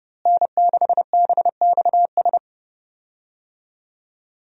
Morse code, heard as a keyed tone, about 30 wpm, 700 Hz. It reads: N6BXS
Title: USA Call Signs - 5 Characters - Repeated 3x - 30wpm